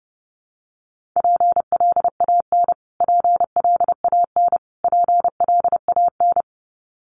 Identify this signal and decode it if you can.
Morse code, heard as a keyed tone, about 30 wpm, 700 Hz. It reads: PLAN PLAN PLAN
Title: Top 500 Words Repeated 30wpm